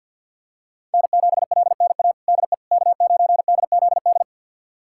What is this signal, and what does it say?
Morse code, heard as a keyed tone, about 51 wpm, 700 Hz. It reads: N7LNA DE K9DZD
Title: USA Callsign DE Callsign - 5 Characters - 50wpm